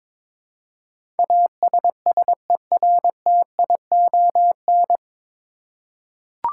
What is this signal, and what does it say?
Morse code, heard as a keyed tone, about 22 wpm, 700 Hz. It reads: ASSERTION
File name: 9 Letter Words - First 3 Letters Given - 22wpm